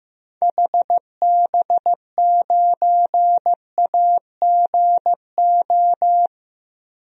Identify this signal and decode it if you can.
Morse code, heard as a keyed tone, about 15 wpm, 700 Hz. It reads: HB9AGO
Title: DXCC-100 Call Signs - 6 Characters - 15wpm